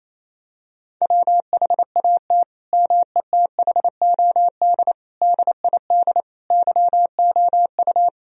Morse code, heard as a keyed tone, about 27 wpm, 700 Hz. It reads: WHAT METHOD DID YOU
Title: Sentences from Top 1000 Words - Review Part 3 - 28wpm